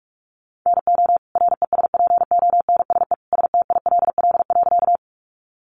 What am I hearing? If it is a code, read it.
Morse code, heard as a keyed tone, about 44 wpm, 700 Hz. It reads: NO RESPONSE STILL.
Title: Sentences from Top 1400 Words 45wpm